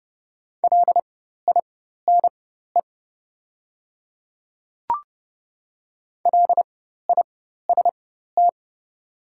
Morse code, heard as a keyed tone, about 30 wpm, spaced as for 14 wpm, 700 Hz. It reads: LINE LIST
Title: 4x Character Spacing - Top 500 Words - 30wpm